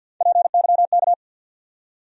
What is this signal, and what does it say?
Morse code, heard as a keyed tone, about 51 wpm, 700 Hz. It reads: PYK
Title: USA Call Sign Postfixes - 3 Letters 50wpm